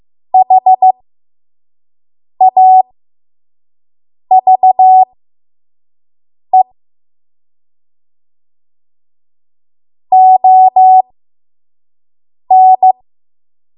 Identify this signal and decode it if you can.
Morse code, heard as a keyed tone, about 15 wpm, spaced as for 5 wpm, 750 Hz. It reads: HAVE ON
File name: Morse / 05WPM